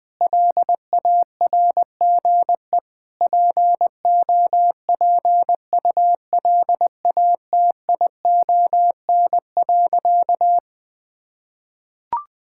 Morse code, heard as a keyed tone, about 20 wpm, 690 Hz. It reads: LARGE POPULATION.
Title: Sentences with 4-Word Phrases - Spoken Beginning - 20wpm